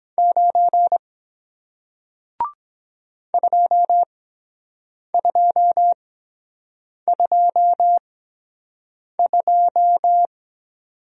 Morse code, heard as a keyed tone, about 23 wpm, spaced as for 7 wpm, 700 Hz. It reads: 9 2222